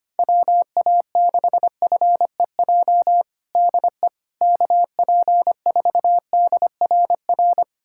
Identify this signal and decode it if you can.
Morse code, heard as a keyed tone, about 25 wpm, 700 Hz. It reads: WA6FEJ DE KP4DRR
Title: USA Callsign DE Callsign - 6 Characters - 25wpm